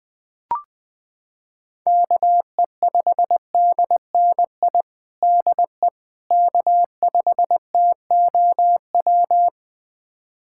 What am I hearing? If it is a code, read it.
Morse code, heard as a keyed tone, about 20 wpm, 700 Hz. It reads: KE5DNI DE K5TOW
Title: USA Callsign DE Callsign 20wpm